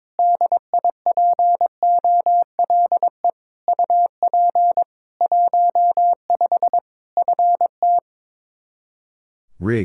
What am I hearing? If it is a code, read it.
Morse code, heard as a keyed tone, about 22 wpm, 700 Hz. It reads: DIPOLE UP 15 FT
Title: Practice QSO - Exchange 2 - Info - 22wpm